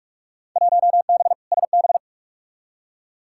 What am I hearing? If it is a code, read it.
Morse code, heard as a keyed tone, about 46 wpm, 700 Hz. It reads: 1B SB
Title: Field Day Exchanges - 45wpm